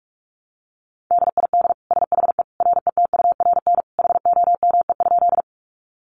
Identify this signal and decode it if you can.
Morse code, heard as a keyed tone, about 44 wpm, 700 Hz. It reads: DID SHE RETURN HOME?
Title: Sentences from Top 800 Words 45wpm